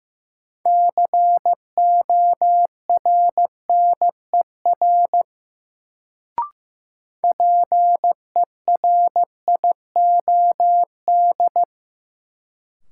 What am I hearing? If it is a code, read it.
Morse code, heard as a keyed tone, about 15 wpm, 700 Hz. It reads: CORNER PERIOD
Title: Sets of 5 Words - New Word Drill - 15wpm